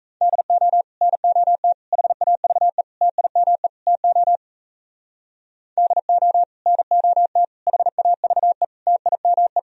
Morse code, heard as a keyed tone, about 34 wpm, 700 Hz. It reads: DO NOT HAVE TIME TO DO NOT HAVE TIME
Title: Speed-Racing - Sets of 5 Words 28wpm